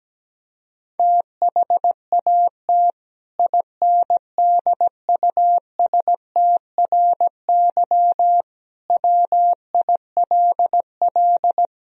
Morse code, heard as a keyed tone, about 17 wpm, 700 Hz. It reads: THAT INDUSTRY WILL
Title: Sentences from Top 1000 Words - Review Part 4 - 17wpm